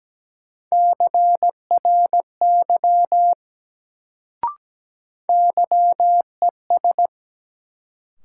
Morse code, heard as a keyed tone, about 17 wpm, 700 Hz. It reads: CRY YES
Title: Sentences from Top 500 Words - New Word Drill - 17wpm